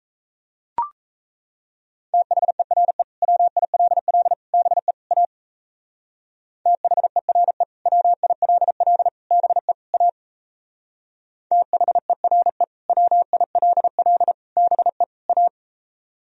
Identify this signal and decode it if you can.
Morse code, heard as a keyed tone, about 35 wpm, 700 Hz. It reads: THERE WILL BE A THERE WILL BE A THERE WILL BE A
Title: Speed-Racing - Sets of 4 Words 28wpm